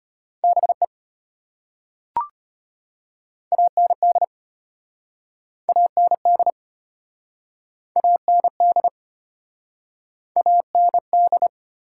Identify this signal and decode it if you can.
Morse code, heard as a keyed tone, about 34 wpm, 700 Hz. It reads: BE AND AND AND AND